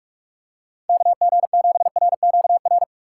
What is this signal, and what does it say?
Morse code, heard as a keyed tone, about 44 wpm, 690 Hz. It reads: KG7RQR
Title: USA Call Signs - 4x Character Spacing - 45wpm